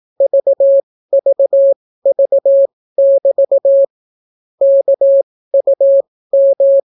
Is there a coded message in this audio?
Yes: Morse code from 0.2 to 6.8 s, heard as a keyed tone, about 18 wpm, spaced as for 14 wpm, 550 Hz. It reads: VVV<BT> KUM